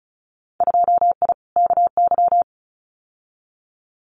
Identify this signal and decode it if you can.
Morse code, heard as a keyed tone, about 36 wpm, 700 Hz. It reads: 2I KY
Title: Winter Field Day Exchanges 35wpm